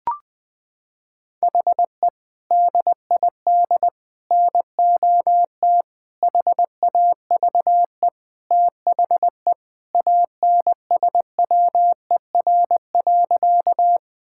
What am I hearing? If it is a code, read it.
Morse code, heard as a keyed tone, about 21 wpm, 700 Hz. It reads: HE DID NOT HAVE THE ANSWER.
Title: Sentences with 4-Word Phrases - 20wpm